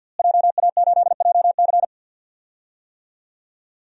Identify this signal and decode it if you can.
Morse code, heard as a keyed tone, about 51 wpm, 700 Hz. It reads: JA8JC
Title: DXCC-100 Call Signs - 5 Characters - Repeated 2x - 50wpm